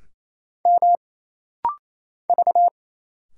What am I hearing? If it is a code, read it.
Morse code, heard as a keyed tone, about 27 wpm, 700 Hz. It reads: M V
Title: Single Letters - Rapid-Fire 28wpm